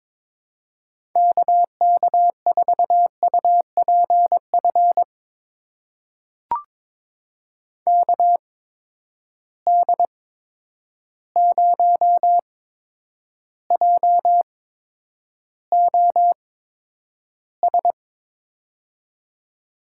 Morse code, heard as a keyed tone, about 22 wpm, spaced as for 9 wpm, 700 Hz. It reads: KK4UPF KD0JOS
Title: USA Call Signs - 6 Characters - 8x Character Spacing - 22wpm